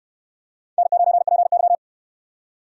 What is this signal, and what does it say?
Morse code, heard as a keyed tone, about 68 wpm, 700 Hz. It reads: N9PY